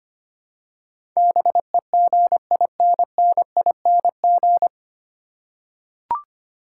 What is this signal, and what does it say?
Morse code, heard as a keyed tone, about 25 wpm, 700 Hz. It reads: BEGINNING
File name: Nine Letter Words - 25wpm